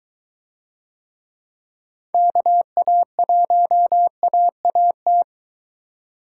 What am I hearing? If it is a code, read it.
Morse code, heard as a keyed tone, about 23 wpm, 700 Hz. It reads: KA1AAT